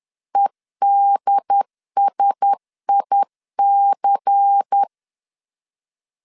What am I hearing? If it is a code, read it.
Morse code, heard as a keyed tone, about 10 wpm, 800 Hz. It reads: EDSIC
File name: LEZIONI / 6 PKX0 / 6 PKX0 10wpm 2707